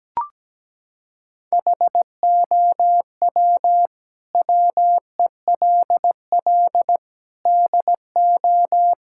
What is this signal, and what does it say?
Morse code, heard as a keyed tone, about 17 wpm, 700 Hz. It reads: HOW WELL DO
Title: Sentences from Top 1000 Words - Review Part 5 - 17wpm